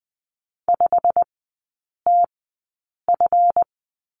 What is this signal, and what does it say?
Morse code, heard as a keyed tone, about 20 wpm, spaced as for 8 wpm, 700 Hz. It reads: 5TF
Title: 138 Lesson 23e - Letter f - Three Characters 20wpm